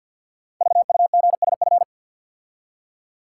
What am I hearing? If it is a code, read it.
Morse code, heard as a keyed tone, about 48 wpm, 690 Hz. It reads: VUGSF
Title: Code Groups - 5 Letters - 50wpm